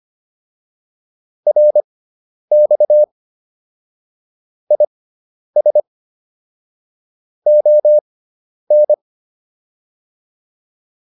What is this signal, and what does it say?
Morse code, heard as a keyed tone, about 25 wpm, spaced as for 10 wpm, 600 Hz. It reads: RX IS ON